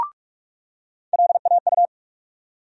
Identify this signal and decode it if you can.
Morse code, heard as a keyed tone, about 42 wpm, 700 Hz. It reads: LAU
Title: USA Call Sign Postfixes - 3 Letters 45wpm